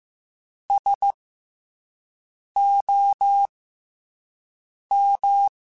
Morse code, heard as a keyed tone, about 15 wpm, 800 Hz. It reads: S O M
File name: Special INGs-mp3s / Special INGs-mp3s / SlowToFaster ing @10